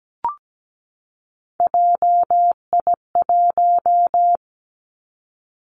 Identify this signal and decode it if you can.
Morse code, heard as a keyed tone, about 17 wpm, 700 Hz. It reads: JI1
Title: Three Challenging Letter-Number Combinations 17wpm